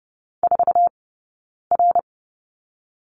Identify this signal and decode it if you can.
Morse code, heard as a keyed tone, about 30 wpm, 700 Hz. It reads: 4 R
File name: 092 Lesson 16d - Letter c - Two Characters 30wpm